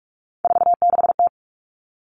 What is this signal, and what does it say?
Morse code, heard as a keyed tone, about 44 wpm, 700 Hz. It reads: 46T
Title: Common Cut Numbers - 3-Digit 45wpm